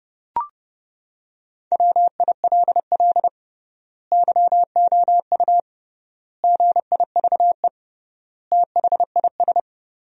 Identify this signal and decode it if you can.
Morse code, heard as a keyed tone, about 32 wpm, 700 Hz. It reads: WILL YOU GIVE THIS
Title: 3x Word-Spacing - Sentences from Top 200 Words - 30wpm